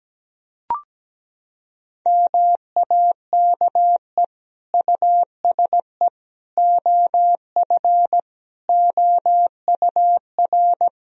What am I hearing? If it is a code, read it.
Morse code, heard as a keyed tone, about 17 wpm, 700 Hz. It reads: MAKE USE OF OUR
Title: Sentences from Top 1000 Words - Review Part 4 - 17wpm